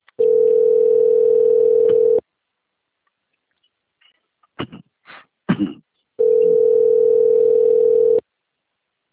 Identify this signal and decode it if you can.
Ringback tone (North America), 440 + 480 Hz.
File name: out- - - - - .39863